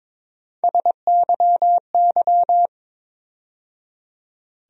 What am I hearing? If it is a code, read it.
Morse code, heard as a keyed tone, about 22 wpm, 700 Hz. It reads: SYY
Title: USA Call Sign Postfixes - 3 Letters 22wpm